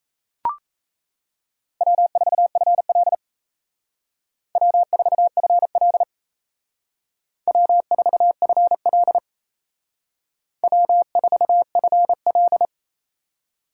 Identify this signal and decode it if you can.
Morse code, heard as a keyed tone, about 36 wpm, 700 Hz. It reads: W4FL W4FL W4FL W4FL